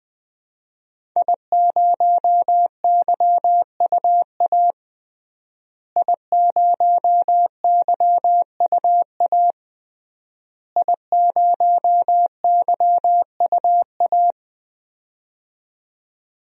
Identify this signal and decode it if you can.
Morse code, heard as a keyed tone, about 20 wpm, 700 Hz. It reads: I0YUA I0YUA I0YUA
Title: DXCC-100 Call Signs - 5 Characters - Repeated 3x - 20wpm